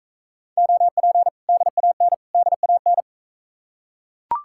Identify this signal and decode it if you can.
Morse code, heard as a keyed tone, about 40 wpm, 700 Hz. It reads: OP DAN DAN